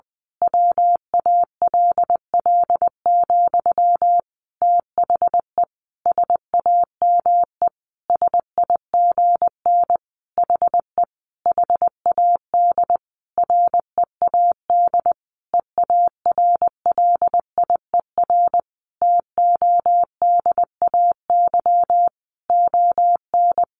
Morse code, heard as a keyed tone, about 20 wpm, 700 Hz. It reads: WALL, THE SAME SIGN HE HAD READ EARLIER TODAY ON
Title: A Sound Of Thunder 20wpm